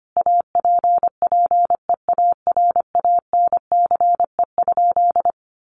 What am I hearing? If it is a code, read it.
Morse code, heard as a keyed tone, about 25 wpm, 700 Hz. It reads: APPEARANCE?
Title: Sentences from Top 1100 Words 25wpm